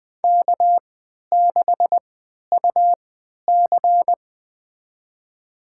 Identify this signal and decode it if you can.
Morse code, heard as a keyed tone, about 20 wpm, spaced as for 11 wpm, 700 Hz. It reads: K6UC
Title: USA Call Signs - 3x Character Spacing - 20wpm